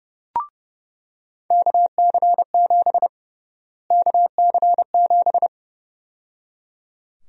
Morse code, heard as a keyed tone, about 30 wpm, 700 Hz. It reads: KC7 KC7
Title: USA Call Sign Prefixes - Repeated 2x - 30wpm